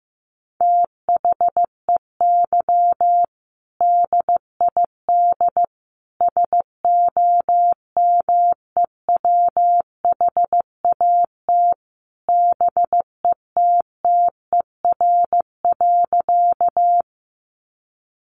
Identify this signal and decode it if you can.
Morse code, heard as a keyed tone, about 14 wpm, 700 Hz. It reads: THEY DID SOMEWHAT BETTER.
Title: Sentences from Top 1300 Words 15wpm